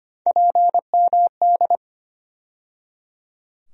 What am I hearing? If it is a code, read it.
Morse code, heard as a keyed tone, about 25 wpm, 700 Hz. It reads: PMD
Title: USA Call Sign Postfixes - 3 Letters 25wpm